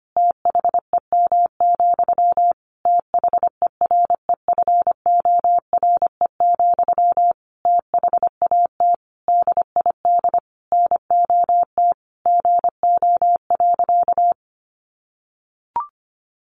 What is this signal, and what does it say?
Morse code, heard as a keyed tone, about 25 wpm, 700 Hz. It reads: THEM, THEREFORE, THAT DID NOT GO.
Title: Sentences from Top 800 Words 25wpm